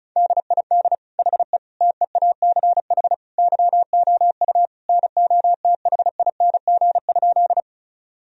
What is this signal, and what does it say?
Morse code, heard as a keyed tone, about 36 wpm, 700 Hz. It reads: DID HE TEACH YOU NOTHING?
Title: Sentences from Top 1000 Words - Review Part 2 - 35wpm